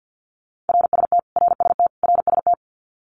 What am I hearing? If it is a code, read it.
Morse code, heard as a keyed tone, about 50 wpm, 700 Hz. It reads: RST RST RST
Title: Abbreviations Repeated 3x - 50wpm